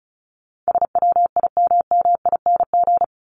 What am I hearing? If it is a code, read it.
Morse code, heard as a keyed tone, about 34 wpm, 700 Hz. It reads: SWIMMING
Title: Top 1501-1750 Words 35wpm